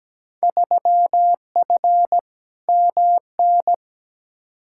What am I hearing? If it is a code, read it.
Morse code, heard as a keyed tone, about 17 wpm, 700 Hz. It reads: 3F MN
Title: Field Day Exchanges Encore - 17wpm